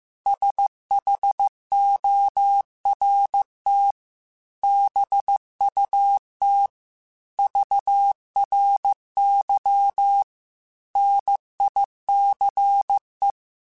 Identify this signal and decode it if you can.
Morse code, heard as a keyed tone, about 15 wpm, 800 Hz. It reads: SHORT BUT VRY NICE
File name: QSOs / QSOs-mp3s / WA7QOX 2@13